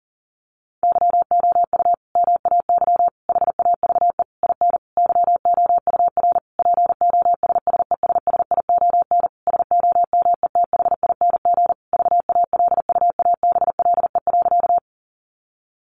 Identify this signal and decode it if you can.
Morse code, heard as a keyed tone, about 40 wpm, 700 Hz. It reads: YOU MAY HAVE IN YOUR POSSESSION SOMETHING VALUABLE.
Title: Sentences from Top 1400 Words 40wpm